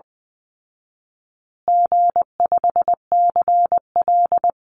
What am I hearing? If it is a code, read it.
Morse code, heard as a keyed tone, about 20 wpm, 700 Hz. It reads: G5CL
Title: Call Signs Repeated 2x - 20wpm